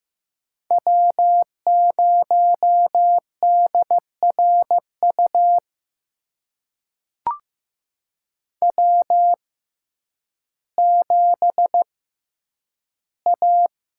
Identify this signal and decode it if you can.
Morse code, heard as a keyed tone, about 15 wpm, spaced as for 9 wpm, 700 Hz. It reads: W0DRU W7A
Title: USA Call Signs - 5 Characters - 6x Character Spacing - 15wpm